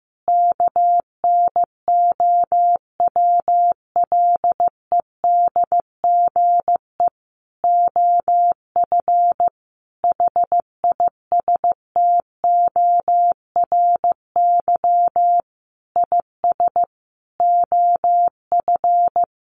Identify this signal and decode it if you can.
Morse code, heard as a keyed tone, about 15 wpm, 700 Hz. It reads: KNOWLEDGE OF HISTORY IS OF
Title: Sentences from Top 900 Words 15wpm